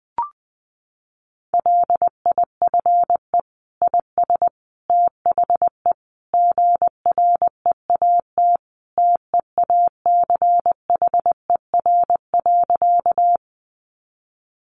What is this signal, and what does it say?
Morse code, heard as a keyed tone, about 20 wpm, 700 Hz. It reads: LIFE IS THE GREAT TEACHER.
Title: Sentences from Top 900 Words 20wpm